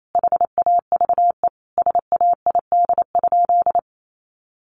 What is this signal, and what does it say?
Morse code, heard as a keyed tone, about 28 wpm, 700 Hz. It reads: HAVE SAID?
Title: Sentences from Top 100 Words 28wpm